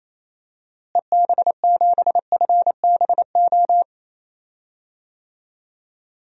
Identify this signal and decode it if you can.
Morse code, heard as a keyed tone, about 27 wpm, 700 Hz. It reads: EB7FBO
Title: DXCC-100 Call Signs - 6 Characters - Repeated 2x - 28wpm